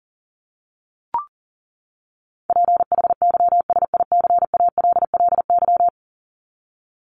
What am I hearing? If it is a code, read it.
Morse code, heard as a keyed tone, about 40 wpm, 700 Hz. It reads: PHYSICALLY
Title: Top 1501-1750 Words 40wpm